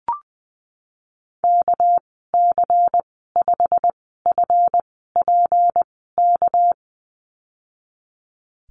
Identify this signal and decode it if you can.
Morse code, heard as a keyed tone, about 20 wpm, 700 Hz. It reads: K C 5 F P K
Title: 238 Lesson 38g - Letter q - Call Signs 20wpm